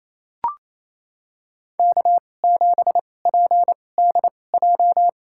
Morse code, heard as a keyed tone, about 28 wpm, spaced as for 20 wpm, 700 Hz. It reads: K7PDJ
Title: USA Call Signs - 5 Characters - 2x Character Spacing - 28wpm